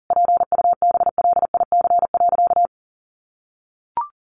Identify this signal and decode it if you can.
Morse code, heard as a keyed tone, about 40 wpm, 700 Hz. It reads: PUBLIC.
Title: Sentences from Top 1500 Words 40wpm